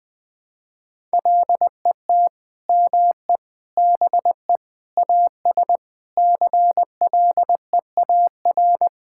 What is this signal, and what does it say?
Morse code, heard as a keyed tone, about 20 wpm, 700 Hz. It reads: LET ME BE AS CLEAR
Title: Sentences with 9-Word Phrases - 20wpm